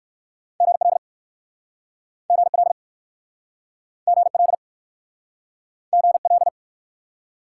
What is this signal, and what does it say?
Morse code, heard as a keyed tone, about 59 wpm, 700 Hz. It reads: GL GL GL GL